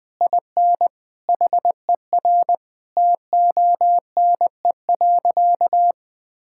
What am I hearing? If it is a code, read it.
Morse code, heard as a keyed tone, about 20 wpm, 700 Hz. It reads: IN HER TONE.
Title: Sentences from Top 1600 Words 20wpm